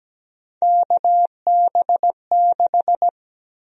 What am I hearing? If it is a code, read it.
Morse code, heard as a keyed tone, about 17 wpm, 700 Hz. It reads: KB6
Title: USA Call Sign Prefixes - 17wpm